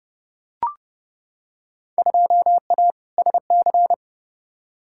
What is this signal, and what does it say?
Morse code, heard as a keyed tone, about 29 wpm, 700 Hz. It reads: 2A SC
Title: Field Day Exchanges - 30wpm